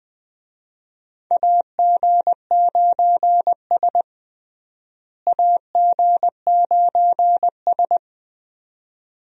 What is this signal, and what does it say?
Morse code, heard as a keyed tone, about 20 wpm, 700 Hz. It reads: AG9S AG9S
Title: USA Call Signs - 4 Characters - Repeated 2x - 20wpm